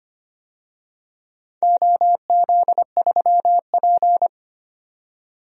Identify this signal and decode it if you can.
Morse code, heard as a keyed tone, about 25 wpm, 700 Hz. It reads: OZ3P
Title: DXCC-100 Call Signs - 4 Characters - 25wpm